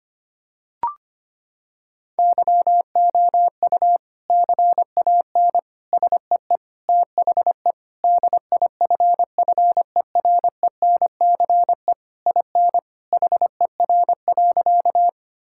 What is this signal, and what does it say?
Morse code, heard as a keyed tone, about 25 wpm, 700 Hz. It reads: YOU CAN SEE THE DIFFERENCE IN HER.
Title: Sentences from Top 1000 Words - Review Part 5 - 25wpm